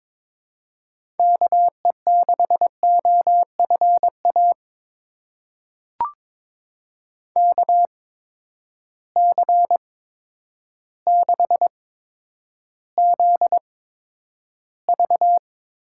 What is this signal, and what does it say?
Morse code, heard as a keyed tone, about 22 wpm, spaced as for 10 wpm, 700 Hz. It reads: KE6OFA KC6ZV
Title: USA Call Signs - 6 Characters - 8x Character Spacing - 22wpm